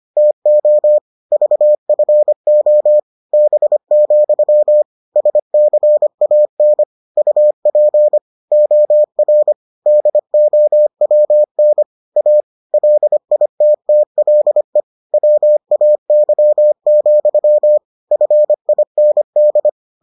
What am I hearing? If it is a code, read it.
Morse code, heard as a keyed tone, about 25 wpm, 600 Hz. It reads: TO VFO B, SCAN UP OR DOWN A LITTLE WAY, FIND